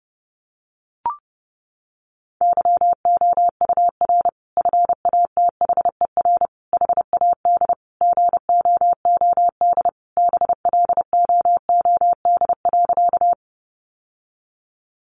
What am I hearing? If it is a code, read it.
Morse code, heard as a keyed tone, about 30 wpm, 700 Hz. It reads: YOUR FATHER HAD GOOD BLOOD.
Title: Sentences from Top 1100 Words 30wpm